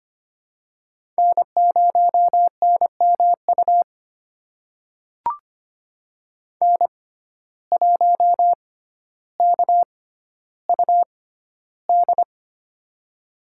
Morse code, heard as a keyed tone, about 25 wpm, spaced as for 13 wpm, 700 Hz. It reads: N0NMU N1KUD
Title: USA Call Signs - 5 Characters - 6x Character Spacing - 25wpm